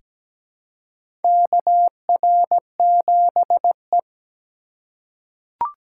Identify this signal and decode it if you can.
Morse code, heard as a keyed tone, about 17 wpm, 700 Hz. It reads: KR7E
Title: USA Call Signs - 4 Characters - 6x Character Spacing - 17wpm